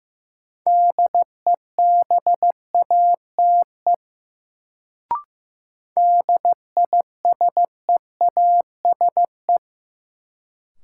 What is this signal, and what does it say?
Morse code, heard as a keyed tone, about 15 wpm, 700 Hz. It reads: DEBATE DISEASE
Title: Sentences from Top 1100 Words - New Word Drill Alt - 15wpm